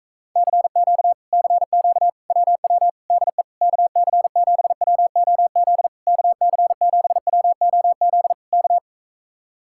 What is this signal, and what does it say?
Morse code, heard as a keyed tone, about 44 wpm, 700 Hz. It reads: CQ CQ WW DE KC7WOZ KC7WOZ K